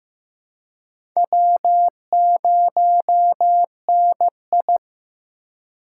Morse code, heard as a keyed tone, about 15 wpm, 700 Hz. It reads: W0NI
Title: USA Call Signs - 4 Characters - 4x Character Spacing - 15wpm